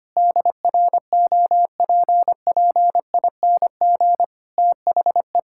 Morse code, heard as a keyed tone, about 25 wpm, 700 Hz. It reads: DROPPING THE
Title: Pangram Challenge v2 - 25wpm